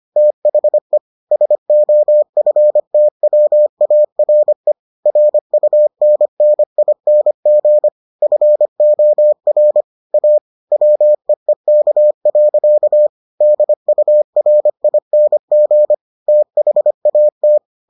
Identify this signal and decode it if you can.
Morse code, heard as a keyed tone, about 25 wpm, 600 Hz. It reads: THE SOFTWARE RUNNING FOR A WEEK. DURING THAT